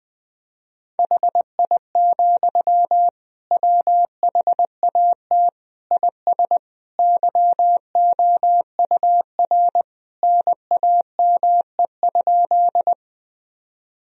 Morse code, heard as a keyed tone, about 21 wpm, 700 Hz. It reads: HI, WHAT IS YOUR NAME?
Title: Sentences from Top 1000 Words - Review Part 3 - 20wpm